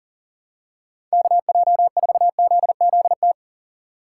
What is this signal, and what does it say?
Morse code, heard as a keyed tone, about 38 wpm, 700 Hz. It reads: KJ4ZZT
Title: USA Call Signs - 2x Character Spacing - 40wpm